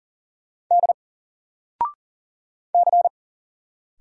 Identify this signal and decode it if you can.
Morse code, heard as a keyed tone, about 40 wpm, 700 Hz. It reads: D C
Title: Single Letter-Number Alt 40wpm